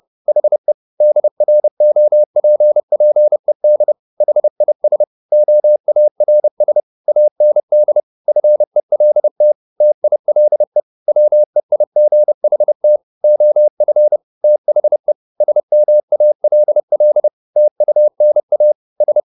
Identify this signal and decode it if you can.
Morse code, heard as a keyed tone, about 30 wpm, 600 Hz. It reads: HE DROPPED HIS OARS AND FELT TILE WEIGHT OF THE SMALL TUNA S
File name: the old man and the sea / the old man and the sea 1 30wpm